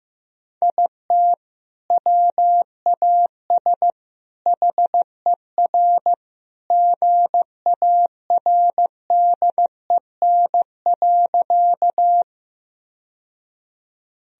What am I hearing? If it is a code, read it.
Morse code, heard as a keyed tone, about 15 wpm, 700 Hz. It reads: IT WAS HER GARDEN.
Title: Sentences from Top 1000 Words - Review Part 4 - 15wpm